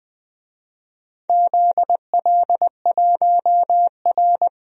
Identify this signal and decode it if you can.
Morse code, heard as a keyed tone, about 20 wpm, 700 Hz. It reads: ZL1R
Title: DXCC-100 Call Signs - 4 Characters - 20wpm